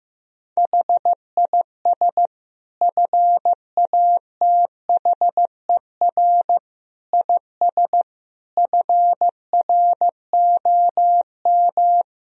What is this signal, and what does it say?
Morse code, heard as a keyed tone, about 16 wpm, 700 Hz. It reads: HIS FATHER IS FROM
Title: Sentences from Top 1000 Words - Review Part 2 - 15wpm